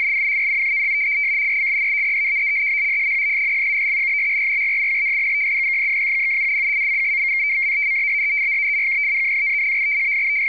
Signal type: two-tone teleprinter signal